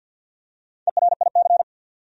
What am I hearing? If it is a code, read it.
Morse code, heard as a keyed tone, about 53 wpm, 700 Hz. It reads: ERIC
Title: USA Common Names - Max 4 Letters - 50wpm